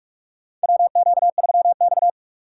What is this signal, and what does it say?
Morse code, heard as a keyed tone, about 43 wpm, 700 Hz. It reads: WQ3X
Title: USA Call Signs - 4 Characters - 6x Character Spacing - 45wpm